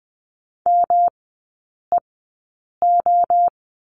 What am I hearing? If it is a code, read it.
Morse code, heard as a keyed tone, about 20 wpm, spaced as for 8 wpm, 700 Hz. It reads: MEO
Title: 198 Lesson 32e - Letter k - Three Characters 20wpm